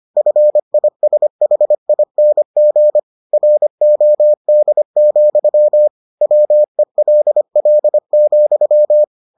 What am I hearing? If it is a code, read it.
Morse code, heard as a keyed tone, about 25 wpm, 600 Hz. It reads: FISHING ROD, WELL,